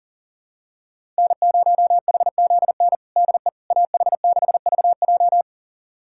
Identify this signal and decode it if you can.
Morse code, heard as a keyed tone, about 40 wpm, 700 Hz. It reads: N0HZN DE AH6VJ
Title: USA Callsign DE Callsign - 5 Characters - 40wpm